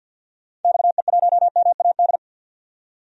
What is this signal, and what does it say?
Morse code, heard as a keyed tone, about 47 wpm, 700 Hz. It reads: XE1MAD
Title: DXCC-100 Call Signs - 6 Characters - 50wpm